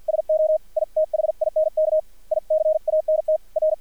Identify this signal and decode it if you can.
Morse code, heard as a keyed tone, about 50 wpm, 630 Hz, with radio noise. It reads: SO I THINK I CANT R
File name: archived / CRSnet